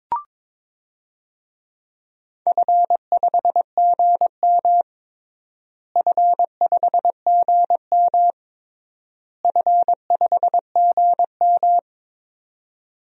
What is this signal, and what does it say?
Morse code, heard as a keyed tone, about 22 wpm, 700 Hz. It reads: F5GM F5GM F5GM
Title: DXCC-100 Call Signs - 4 Characters - Repeated 3x - 22wpm